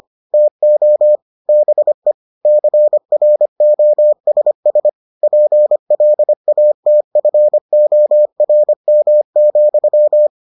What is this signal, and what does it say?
Morse code, heard as a keyed tone, about 25 wpm, 600 Hz. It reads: TO BE CROSS PLATFORM,